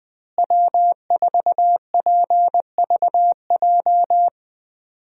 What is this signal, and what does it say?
Morse code, heard as a keyed tone, about 20 wpm, 700 Hz. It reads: W4PVJ
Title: USA Call Signs - 5 Characters - Repeated 2x - 20wpm